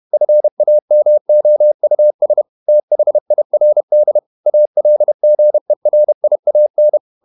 Morse code, heard as a keyed tone, about 31 wpm, 600 Hz. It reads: FAMOUS THIRD ALGERIAN
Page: Never 31wpm | CW med Gnister